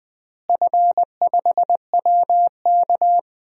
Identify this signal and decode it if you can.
Morse code, heard as a keyed tone, about 21 wpm, 700 Hz. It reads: F5WK
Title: DXCC-100 Call Signs - 4 Characters - Repeated 2x - 20wpm